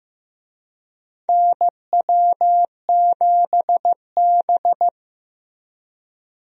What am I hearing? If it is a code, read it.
Morse code, heard as a keyed tone, about 15 wpm, 700 Hz. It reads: NW7B
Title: USA Call Signs - 4 Characters - Repeated 2x - 15wpm